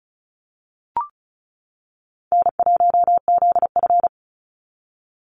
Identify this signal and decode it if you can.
Morse code, heard as a keyed tone, about 34 wpm, 700 Hz. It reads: N1ZF
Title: NA SOTA Activator Callsigns 35wpm